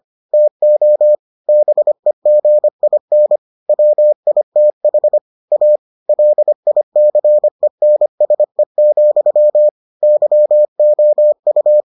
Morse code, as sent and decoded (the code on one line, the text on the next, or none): - --- / -... --. .. -. / .-- .. - .... / .- / .-.. .. -.-. -. ... --..-- / -.-- --- ..-
TO BEGIN WITH A LICENSE, YOU